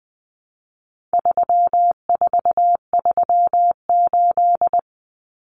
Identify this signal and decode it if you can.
Morse code, heard as a keyed tone, about 20 wpm, 700 Hz. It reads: 3438
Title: Numbers - 4-Digit 20wpm